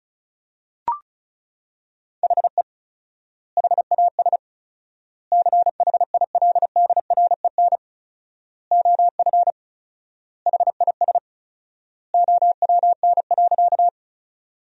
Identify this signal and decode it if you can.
Morse code, heard as a keyed tone, about 36 wpm, 700 Hz. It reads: HE HAS CHILDREN OF HIS OWN.
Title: 4x Word-Spacing - Sentences from Top 300 Words - 35wpm